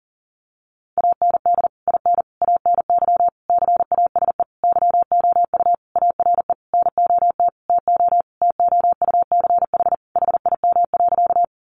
Morse code, heard as a keyed tone, about 40 wpm, 700 Hz. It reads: AND IN ANY CASE YOU ARE NOT TO TOUCH HIM.
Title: Sentences from Top 1200 Words 40wpm